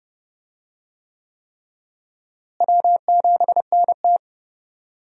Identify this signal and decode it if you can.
Morse code, heard as a keyed tone, about 31 wpm, 700 Hz. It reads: W7NT
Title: USA Call Signs - 4 Characters - Repeated 3x - 30wpm